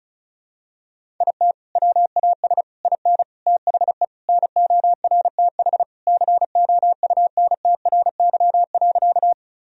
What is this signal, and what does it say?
Morse code, heard as a keyed tone, about 36 wpm, 700 Hz. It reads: IT WAS IN THE NORTH COUNTRY.
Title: Sentences from Top 1000 Words - Review Part 1 - 35wpm